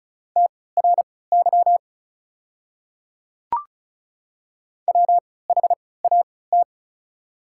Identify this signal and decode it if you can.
Morse code, heard as a keyed tone, about 36 wpm, spaced as for 20 wpm, 700 Hz. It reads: TRY WHAT
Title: 3x Character Spacing - Top 200 Words - 35wpm